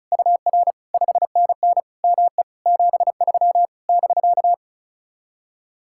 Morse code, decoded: UR 5NN ME 73 <BK>

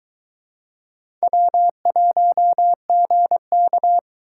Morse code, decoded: W1GK

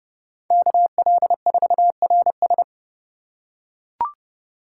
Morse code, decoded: KL4RS

